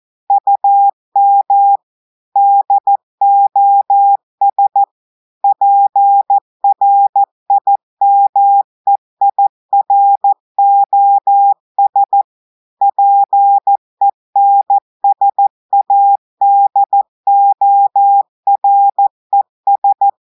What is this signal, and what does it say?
Morse code, heard as a keyed tone, about 14 wpm, 800 Hz. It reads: UM DOS PRIMEIROS PENSADORES
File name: audios / portugais / portugais texte en clair 1 vitesse 14 mots minutes